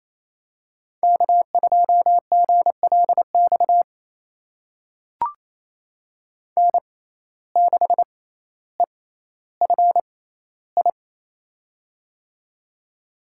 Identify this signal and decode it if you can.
Morse code, heard as a keyed tone, about 27 wpm, spaced as for 14 wpm, 700 Hz. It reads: K2GLX N6EFI
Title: USA Call Signs - 5 Characters - 6x Character Spacing - 28wpm